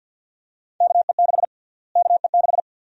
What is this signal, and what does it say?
Morse code, heard as a keyed tone, about 51 wpm, 700 Hz. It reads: KE6 KE6
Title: USA Call Sign Prefixes - Repeated 2x - 50wpm